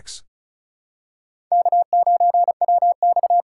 Morse code, heard as a keyed tone, about 36 wpm, 700 Hz. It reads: K9WX
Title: USA Call Signs - 4 Characters - 3x Character Spacing - 35wpm